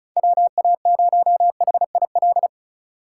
Morse code, heard as a keyed tone, about 46 wpm, 700 Hz. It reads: WA0HIL